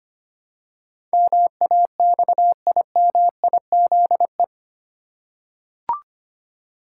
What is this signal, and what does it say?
Morse code, heard as a keyed tone, about 25 wpm, 700 Hz. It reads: MAXIMIZE
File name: Eight Letter Words - 25wpm